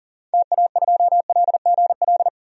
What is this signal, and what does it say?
Morse code, heard as a keyed tone, about 40 wpm, 700 Hz. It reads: TA2LGL
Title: DXCC-100 Call Signs - 6 Characters - 40wpm